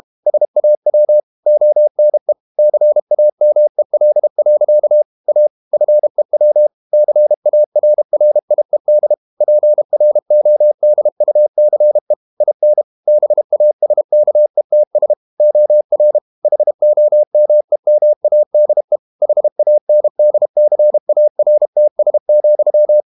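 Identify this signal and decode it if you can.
Morse code, heard as a keyed tone, about 32 wpm, 600 Hz. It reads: SAW ONE CAMEL. A FEW CARRIED PRODUCE IN BASKETS OR HOMEMADE HANDCARTS,
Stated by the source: Morse code practice at 32 wpm from CW med Gnister, reading Ken Follett's novel Never in English.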